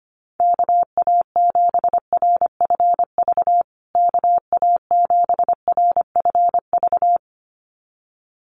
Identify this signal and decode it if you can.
Morse code, heard as a keyed tone, about 25 wpm, 700 Hz. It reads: KA7RFV KA7RFV
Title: USA Call Signs Repeated 2x 25wpm